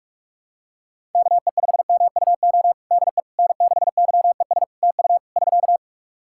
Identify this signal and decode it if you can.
Morse code, heard as a keyed tone, about 48 wpm, 700 Hz. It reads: KE5MUO DE N6YES TU <SK>
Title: K1USN SST QSO Practice 45wpm